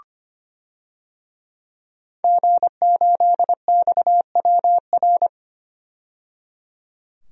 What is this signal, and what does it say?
Morse code, heard as a keyed tone, about 25 wpm, 700 Hz. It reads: G8XWR
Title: DXCC-100 Call Signs - 5 Characters - 25wpm